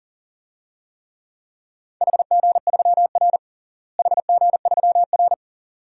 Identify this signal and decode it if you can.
Morse code, heard as a keyed tone, about 37 wpm, 700 Hz. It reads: HG3R HG3R
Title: DXCC-100 Call Signs - 4 Characters - Repeated 2x - 40wpm